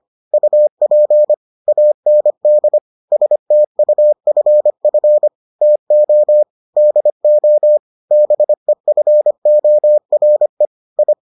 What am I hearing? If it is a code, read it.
Morse code, heard as a keyed tone, about 25 wpm, 600 Hz. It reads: UP AND STUFF TO DO BEFORE I